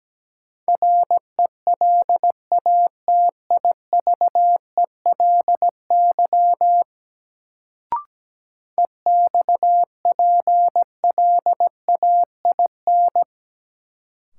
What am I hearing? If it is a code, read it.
Morse code, heard as a keyed tone, about 17 wpm, 700 Hz. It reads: RELATIVELY EXPLAIN
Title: Sentences from Top 1100 Words - New Word Drill - 17wpm